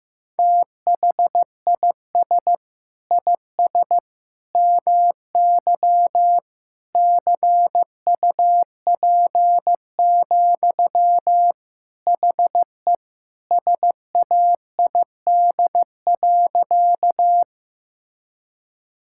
Morse code, heard as a keyed tone, about 15 wpm, 700 Hz. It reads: THIS IS MY CUP, HE SAID.
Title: Sentences from Top 1600 Words 15wpm